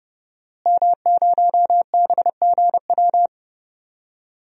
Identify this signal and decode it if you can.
Morse code, heard as a keyed tone, about 30 wpm, 700 Hz. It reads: M0BGW
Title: DXCC-100 Call Signs - 5 Characters - Repeated 2x - 30wpm